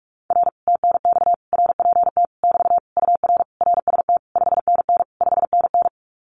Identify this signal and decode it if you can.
Morse code, heard as a keyed tone, about 46 wpm, 700 Hz. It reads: R TNX RPT <BT> UR RST 5NN 5NN
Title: Common QSO Phrases Encore 45wpm